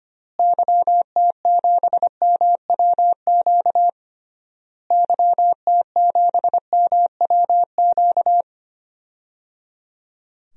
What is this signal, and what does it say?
Morse code, heard as a keyed tone, about 25 wpm, 700 Hz. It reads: YT7MWQ YT7MWQ
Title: DXCC-100 Call Signs - 6 Characters - Repeated 3x - 25wpm